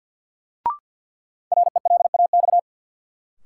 Morse code, heard as a keyed tone, about 48 wpm, 700 Hz. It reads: RELAX
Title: Sentences from Top 1400 Words - New Word Drill - 50wpm